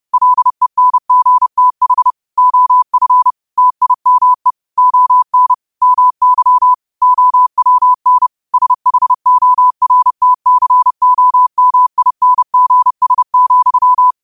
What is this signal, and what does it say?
Morse code, heard as a keyed tone, about 30 wpm, 1000 Hz. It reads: LENGTH OF TIME ON MY OWN SHORTCOMINGS,
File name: qotdfiles / 30wpm